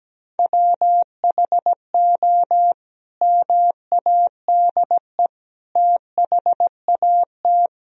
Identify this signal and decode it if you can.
Morse code, heard as a keyed tone, about 17 wpm, 700 Hz. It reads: WHO MADE THAT
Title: Sentences from Top 1000 Words - Review Part 2 - 17wpm